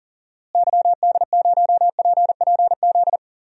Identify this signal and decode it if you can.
Morse code, heard as a keyed tone, about 40 wpm, 700 Hz. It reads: YD0PPZ
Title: DXCC-100 Call Signs - 6 Characters - Repeated 3x - 40wpm